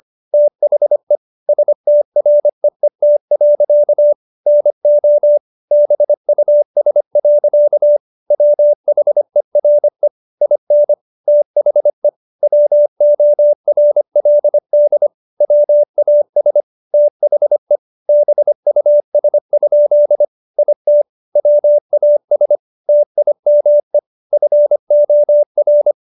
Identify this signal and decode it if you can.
Morse code, heard as a keyed tone, about 25 wpm, 590 Hz. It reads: THE STREET. NO BUS. WHERE IN THE WORLD WAS THE BUS? IT WAS TIME FOR